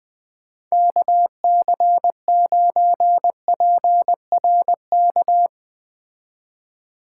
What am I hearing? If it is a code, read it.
Morse code, heard as a keyed tone, about 20 wpm, 700 Hz. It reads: KC9PRK